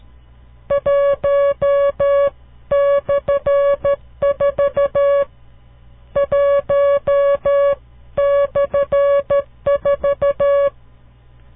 Morse code, heard as a keyed tone, about 12 wpm, 560 Hz. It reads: 1/4 1/4
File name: CW Learning Programs / farnsworth / unit 5